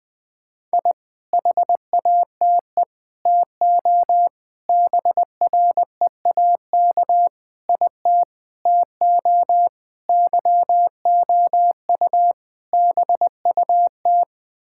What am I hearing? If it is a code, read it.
Morse code, heard as a keyed tone, about 19 wpm, 700 Hz. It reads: I HATE TO BREAK IT TO YOU BUT
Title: Sets of 8 Words - 20wpm